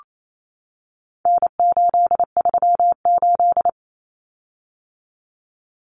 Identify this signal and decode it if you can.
Morse code, heard as a keyed tone, about 28 wpm, 700 Hz. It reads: N838
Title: Common Cut Numbers - 4-Digit 28wpm